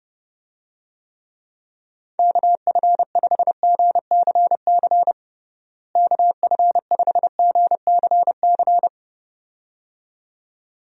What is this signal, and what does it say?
Morse code, heard as a keyed tone, about 30 wpm, 700 Hz. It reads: KF5GCC KF5GCC
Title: USA Call Signs - 6 Characters - Repeated 2x - 30wpm